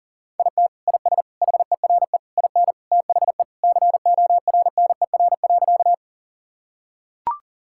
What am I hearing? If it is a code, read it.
Morse code, heard as a keyed tone, about 40 wpm, 700 Hz. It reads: IT IS HERE IN THE CORNER.
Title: Sentences from Top 1600 Words 40wpm